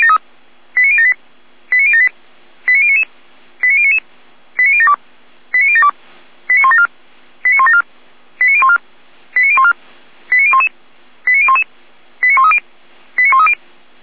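Five-tone selective call (ZVEI), heard as quick runs of five tones, tones of about 70 ms. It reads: 89078 89078 89099 89099 89072 89072 89174 89174 89014 89014 89011 89011 89130 89130
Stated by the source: Five-tone ZVEI selective call received by SDR.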